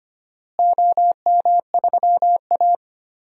Morse code, heard as a keyed tone, about 25 wpm, 700 Hz. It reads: OM3A
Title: DXCC-100 Call Signs - 4 Characters - Repeated 3x - 25wpm